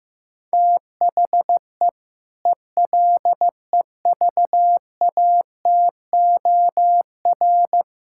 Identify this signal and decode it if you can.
Morse code, heard as a keyed tone, about 15 wpm, 700 Hz. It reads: THE ELEVATOR
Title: Sentences from Top 1600 Words 15wpm